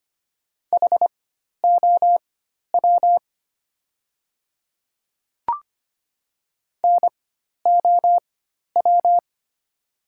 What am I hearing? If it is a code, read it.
Morse code, heard as a keyed tone, about 25 wpm, spaced as for 12 wpm, 700 Hz. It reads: HOW NOW
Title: 4x Character Spacing - Top 100 Words - 25wpm